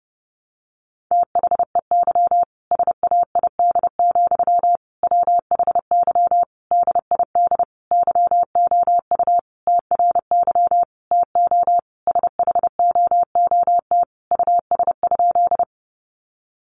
Morse code, heard as a keyed tone, about 30 wpm, 700 Hz. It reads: THEY SAID, WHY DID YOU TRY TO SHOOT US?
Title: Sentences from Top 1200 Words 30wpm